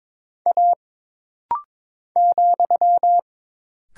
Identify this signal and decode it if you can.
Morse code, heard as a keyed tone, about 22 wpm, 700 Hz. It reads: A ,
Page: This rapid-fire timing is geared to push your ICR (Instant Character Recognition) skills.